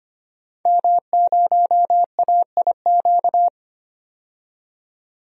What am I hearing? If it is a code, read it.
Morse code, heard as a keyed tone, about 25 wpm, 700 Hz. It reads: M0AIQ
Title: DXCC-100 Call Signs - 5 Characters - 25wpm